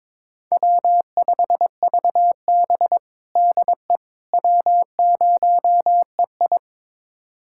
Morse code, heard as a keyed tone, about 21 wpm, 700 Hz. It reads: W5VB DE W0EI
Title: USA Callsign DE Callsign - 4 Characters - 22wpm